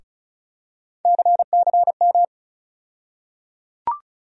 Morse code, heard as a keyed tone, about 36 wpm, 700 Hz. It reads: CCM
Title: USA Call Sign Postfixes - 3 Letters 35wpm